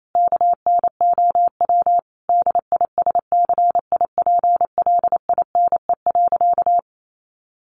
Morse code, heard as a keyed tone, about 28 wpm, 700 Hz. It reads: KNOW DISCIPLINE.
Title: Sentences from Top 900 Words 28wpm